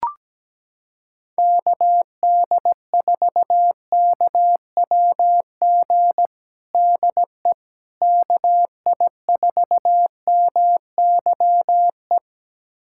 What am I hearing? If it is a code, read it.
Morse code, heard as a keyed tone, about 17 wpm, 700 Hz. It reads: KD4KWG DE KI4MYE
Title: USA Callsign DE Callsign 17wpm